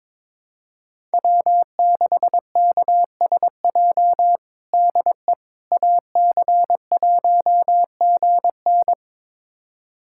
Morse code, heard as a keyed tone, about 22 wpm, 700 Hz. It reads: W6KSJ DE AC1GN
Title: USA Callsign DE Callsign - 5 Characters - 22wpm